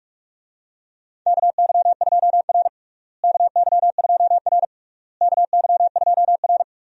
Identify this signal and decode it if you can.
Morse code, heard as a keyed tone, about 47 wpm, 700 Hz. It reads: KY2R KY2R KY2R
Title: USA Call Signs - 4 Characters - Repeated 3x - 45wpm